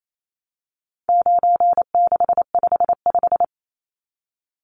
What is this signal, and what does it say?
Morse code, heard as a keyed tone, about 28 wpm, 700 Hz. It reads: 9655
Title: Numbers - 4-Digit 28wpm